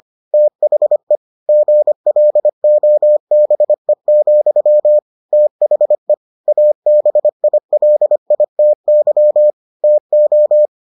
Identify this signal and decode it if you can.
Morse code, heard as a keyed tone, about 25 wpm, 600 Hz. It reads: THE GLOBE, THE ABILITY TO